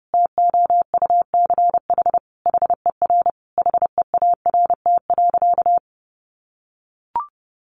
Morse code, heard as a keyed tone, about 28 wpm, 700 Hz. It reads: TOUCH HER HEART.
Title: Sentences from Top 1200 Words 30wpm